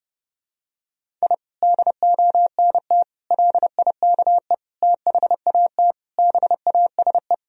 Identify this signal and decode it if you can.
Morse code, heard as a keyed tone, about 30 wpm, 700 Hz. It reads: I DONT LIKE THAT BASE
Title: Sentences from Top 1000 Words - Review Part 1 - 30wpm